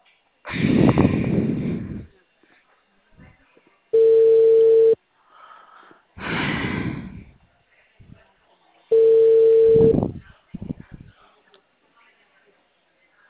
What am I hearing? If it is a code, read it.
Ringback tone (China, 450 Hz).